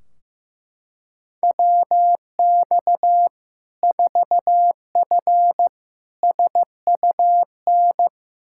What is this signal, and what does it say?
Morse code, heard as a keyed tone, about 15 wpm, 700 Hz. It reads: WX 4F SUN
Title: Practice QSO - Exchange 2 - Info - 15wpm